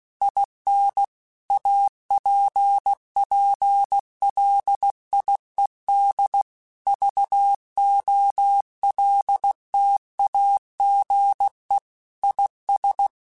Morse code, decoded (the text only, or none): IN APPLIED VOLTAGE IS